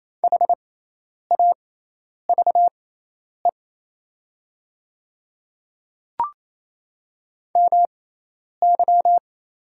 Morse code, heard as a keyed tone, about 27 wpm, spaced as for 10 wpm, 700 Hz. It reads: HAVE MY
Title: 6x Character Spacing - Top 100 Words - 28wpm